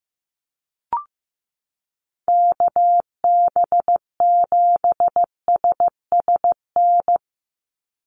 Morse code, heard as a keyed tone, about 15 wpm, 700 Hz. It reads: KB7SSN
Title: USA Call Signs 15wpm